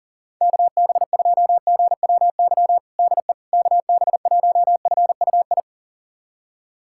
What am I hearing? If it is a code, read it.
Morse code, heard as a keyed tone, about 40 wpm, 700 Hz. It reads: KB2GWY DE KB1FUI
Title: USA Callsign DE Callsign - 6 Characters - 40wpm